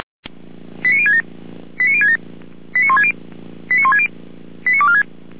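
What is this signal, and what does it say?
Five-tone selective call (ZVEI), heard as quick runs of five tones, tones of about 70 ms. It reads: 89067 89067 89160 89160 89246
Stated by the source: Five-tone ZVEI selective call received on an SDR.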